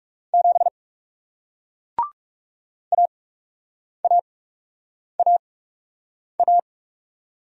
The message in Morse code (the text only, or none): 7 AAAA